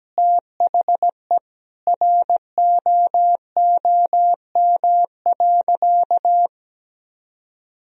Morse code, heard as a keyed tone, about 17 wpm, 700 Hz. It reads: THE ROOM.
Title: Sentences from Top 1000 Words - Review Part 1 - 17wpm